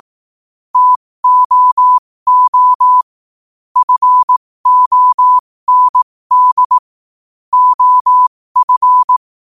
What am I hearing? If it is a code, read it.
Morse code, heard as a keyed tone, about 18 wpm, spaced as for 15 wpm, 1000 Hz. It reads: TOO FOND OF
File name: qotdfiles / 15wpm